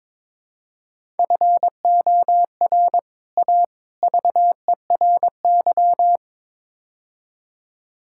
Morse code, heard as a keyed tone, about 21 wpm, 700 Hz. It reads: FOR A VERY
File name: practice / 22wpm / Build A Phrase - 5-Word Phrases - 22wpm